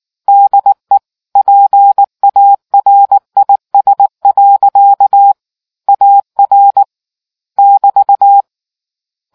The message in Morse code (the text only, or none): DE PARIS. AR <BT>